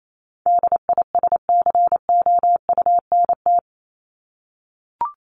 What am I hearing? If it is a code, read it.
Morse code, heard as a keyed tone, about 28 wpm, 700 Hz. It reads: DISCOUNT
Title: Top 1001-1250 Words 28wpm